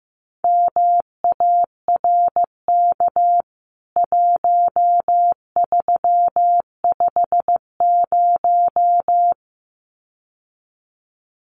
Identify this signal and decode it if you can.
Morse code, heard as a keyed tone, about 15 wpm, 700 Hz. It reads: MARK 1350
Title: CWT Exchange 15wpm